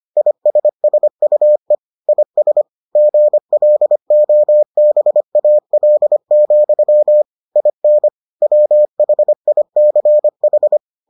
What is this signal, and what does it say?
Morse code, heard as a keyed tone, about 25 wpm, 600 Hz. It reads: ISSUE IS GLOBAL, IN WHICH